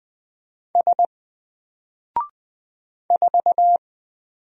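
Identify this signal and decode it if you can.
Morse code, heard as a keyed tone, about 20 wpm, 700 Hz. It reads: S 4
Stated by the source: Single letter and number Morse drill at 20 wpm, from N4LCD.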